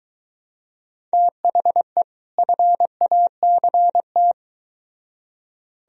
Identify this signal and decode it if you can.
Morse code, heard as a keyed tone, about 22 wpm, 700 Hz. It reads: THE FACT